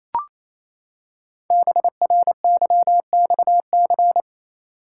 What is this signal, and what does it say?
Morse code, heard as a keyed tone, about 29 wpm, 700 Hz. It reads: BRYXC
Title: Code Groups - 5 Letters - 28wpm